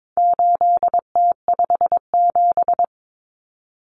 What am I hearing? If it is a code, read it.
Morse code, heard as a keyed tone, about 22 wpm, 700 Hz. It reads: 8T57
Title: Common Cut Numbers - 4-Digit 22wpm